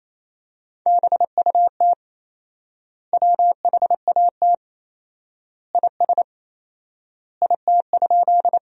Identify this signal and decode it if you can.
Morse code, heard as a keyed tone, about 28 wpm, 700 Hz. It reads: BUT WHAT IS IT?
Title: 4x Word-Spacing - Sentences from Top 100 Words - 28wpm